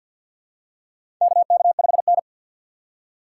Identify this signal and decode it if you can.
Morse code, heard as a keyed tone, about 46 wpm, 700 Hz. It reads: KK5N
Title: USA Call Signs - 4 Characters - 6x Character Spacing - 50wpm